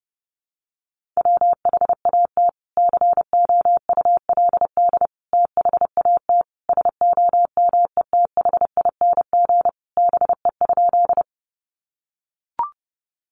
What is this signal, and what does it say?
Morse code, heard as a keyed tone, about 30 wpm, 700 Hz. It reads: WHAT COULD THAT SOMETHING BE?
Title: Sentences from Top 600 Words 30wpm